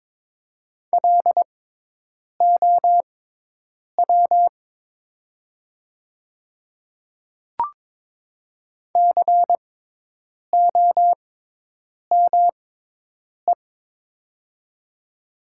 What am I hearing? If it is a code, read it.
Morse code, heard as a keyed tone, about 22 wpm, spaced as for 8 wpm, 700 Hz. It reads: LOW COME
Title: 6x Character Spacing - Top 200 Words - 22wpm